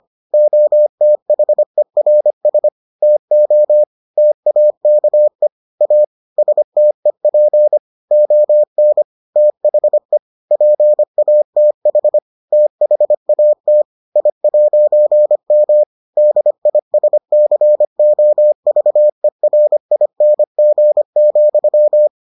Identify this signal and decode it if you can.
Morse code, heard as a keyed tone, about 25 wpm, 600 Hz. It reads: OTHERS TO TAKE A STEP ON THE PATH THAT I'M DISCOVERING,